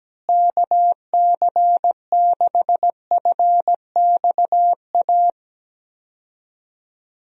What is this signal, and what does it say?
Morse code, heard as a keyed tone, about 17 wpm, 700 Hz. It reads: KC6FXA
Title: USA Call Signs - 6 Characters - Repeated 2x - 17wpm